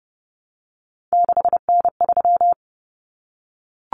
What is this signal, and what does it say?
Morse code, heard as a keyed tone, about 30 wpm, 700 Hz. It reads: 6N3
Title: Common Cut Numbers - 3-Digit 30wpm